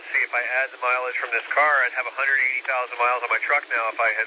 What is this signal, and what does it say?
Emergency Alert System (SAME) tones, heard as harsh buzzing bursts, with speech. It is heard from 1.1 to 3.8 s.